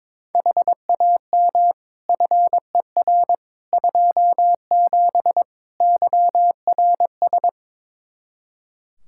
Morse code, heard as a keyed tone, about 22 wpm, 700 Hz. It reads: HAM FER 27 YRS
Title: Practice QSO - Exchange 3 - More Info - 22wpm